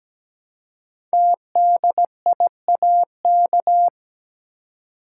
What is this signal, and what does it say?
Morse code, heard as a keyed tone, about 17 wpm, 700 Hz. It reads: TDIAK
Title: Code Groups - 5 Letters - 17wpm